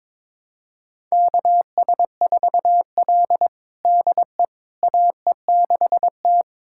Morse code, heard as a keyed tone, about 22 wpm, 700 Hz. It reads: KS4L DE AE6T
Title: USA Callsign DE Callsign - 4 Characters - 22wpm